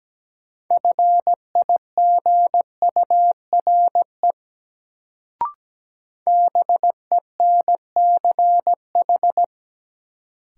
Morse code, heard as a keyed tone, about 17 wpm, 700 Hz. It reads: FIGURE BENCH
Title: Sentences from Top 1400 Words - New Word Drill Alt - 17wpm